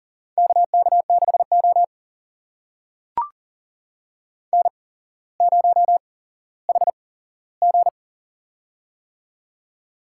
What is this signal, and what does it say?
Morse code, heard as a keyed tone, about 38 wpm, spaced as for 17 wpm, 690 Hz. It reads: KK6O N0HG
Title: USA Call Signs - 4 Characters - 8x Character Spacing - 40wpm